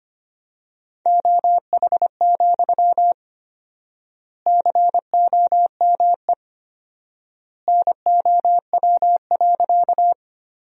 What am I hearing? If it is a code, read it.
Morse code, heard as a keyed tone, about 25 wpm, 700 Hz. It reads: OH, COME NOW.
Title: 4x Word-Spacing - Sentences from Top 500 Words - 25wpm